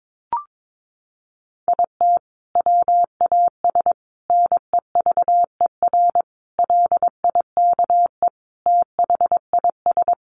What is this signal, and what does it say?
Morse code, heard as a keyed tone, about 22 wpm, 700 Hz. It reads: IT WAS NEVER LIKE THIS
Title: A Sound Of Thunder 22wpm